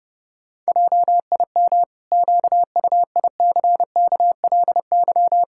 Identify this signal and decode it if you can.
Morse code, heard as a keyed tone, about 32 wpm, 700 Hz. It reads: JIM QUICKLY
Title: Pangram Challenge v2 - 30wpm